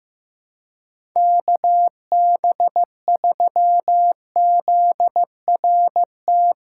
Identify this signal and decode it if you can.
Morse code, heard as a keyed tone, about 15 wpm, 700 Hz. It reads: KB3ZRT
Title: USA Callsign DE Callsign 15wpm